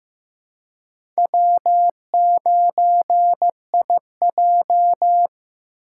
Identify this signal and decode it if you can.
Morse code, heard as a keyed tone, about 15 wpm, 700 Hz. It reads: W9IJ
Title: USA Call Signs - 4 Characters - 4x Character Spacing - 15wpm